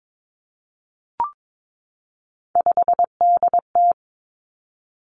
Morse code, heard as a keyed tone, about 22 wpm, 700 Hz. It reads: 5DT
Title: Three Challenging Letter-Number Combinations 22wpm